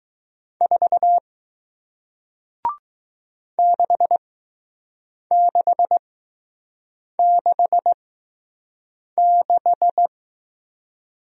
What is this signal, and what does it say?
Morse code, heard as a keyed tone, about 21 wpm, spaced as for 6 wpm, 700 Hz. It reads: 4 6666